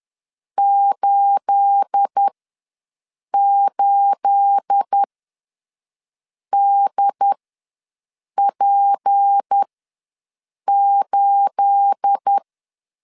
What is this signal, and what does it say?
Morse code, heard as a keyed tone, about 10 wpm, spaced as for 5 wpm, 800 Hz. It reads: 88DP8
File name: LEZIONI / 9 28 ws 10wpm 4124